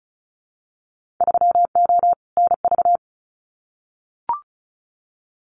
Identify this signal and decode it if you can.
Morse code, heard as a keyed tone, about 34 wpm, 700 Hz. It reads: 3O NV
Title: Winter Field Day Exchanges Encore 35wpm